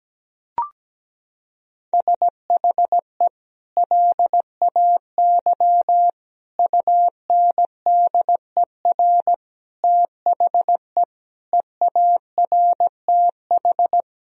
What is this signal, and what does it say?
Morse code, heard as a keyed tone, about 17 wpm, 700 Hz. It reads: SHE LAY UNDER THE EARTH
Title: Sentences from Top 1000 Words - Review Part 2 - 17wpm